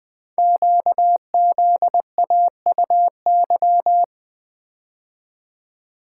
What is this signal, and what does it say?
Morse code, heard as a keyed tone, about 20 wpm, 700 Hz. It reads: QZAUY
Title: Code Groups - 5 Letters - 20wpm